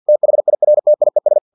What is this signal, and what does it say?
Morse code, heard as a keyed tone, about 50 wpm, 600 Hz. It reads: THIRTIES
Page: Never 49wpm